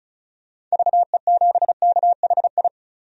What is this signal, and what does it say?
Morse code, heard as a keyed tone, about 36 wpm, 700 Hz. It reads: VE7KHI
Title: DXCC-100 Call Signs - 6 Characters - 35wpm